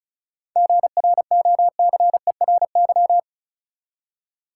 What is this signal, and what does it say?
Morse code, heard as a keyed tone, about 36 wpm, 700 Hz. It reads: GROCERY
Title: Seven Letter Words - 35wpm